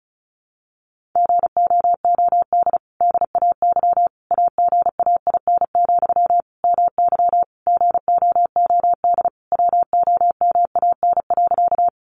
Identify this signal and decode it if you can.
Morse code, heard as a keyed tone, about 34 wpm, 700 Hz. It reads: GOOD DAY AGAIN, MY GOOD WOMAN.
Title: Sentences from Top 1100 Words 35wpm